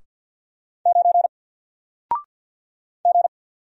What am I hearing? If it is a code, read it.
Morse code, heard as a keyed tone, about 50 wpm, 700 Hz. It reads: 9 G
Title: Single Letter-Number 50wpm